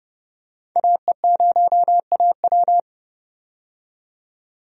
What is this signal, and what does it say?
Morse code, heard as a keyed tone, about 30 wpm, 700 Hz. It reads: AE0AW
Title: USA Call Signs - 5 Characters - 30wpm